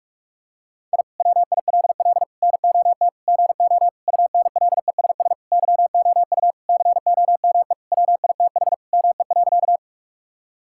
Morse code, heard as a keyed tone, about 46 wpm, 700 Hz. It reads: I WILL NOT GO UNLESS YOU COME WITH ME.
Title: Sentences from Top 1000 Words - Review Part 6 - 45wpm